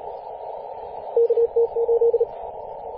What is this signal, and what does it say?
Morse code, heard as a keyed tone, about 35 wpm, 470 Hz, with radio noise. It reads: KT8
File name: posts / 2016-apr-22-2153z-kh8-kc0w-18-080mhz-cw